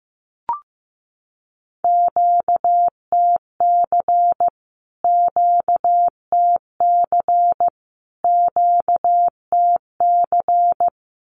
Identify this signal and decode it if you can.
Morse code, heard as a keyed tone, about 15 wpm, 700 Hz. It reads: QTC QTC QTC
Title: Q-Codes Repeated 3x 15wpm